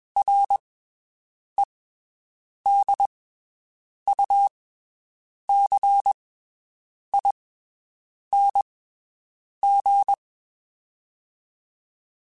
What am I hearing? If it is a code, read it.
Morse code, heard as a keyed tone, about 21 wpm, spaced as for 7 wpm, 800 Hz. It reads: REDUCING
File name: (spaced) / QSOs Spaced-mp3s / WD0FBT 1 @20 sp